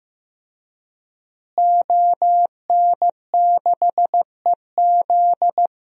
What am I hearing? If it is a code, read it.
Morse code, heard as a keyed tone, about 16 wpm, 700 Hz. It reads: ON6EZ
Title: DXCC-100 Call Signs - 5 Characters - Repeated 3x - 15wpm